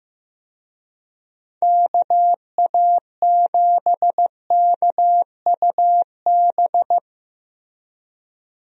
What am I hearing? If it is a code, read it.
Morse code, heard as a keyed tone, about 15 wpm, 700 Hz. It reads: KA7KUB
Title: USA Call Signs - 6 Characters - Repeated 2x - 15wpm